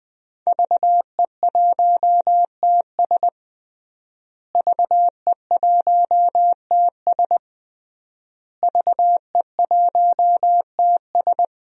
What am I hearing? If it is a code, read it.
Morse code, heard as a keyed tone, about 20 wpm, 700 Hz. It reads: VE1TS VE1TS VE1TS
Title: DXCC-100 Call Signs - 5 Characters - Repeated 3x - 20wpm